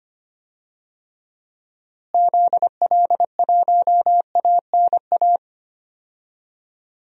Morse code, heard as a keyed tone, about 25 wpm, 700 Hz. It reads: ZL1ANA
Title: DXCC-100 Call Signs - 6 Characters - Repeated 3x - 25wpm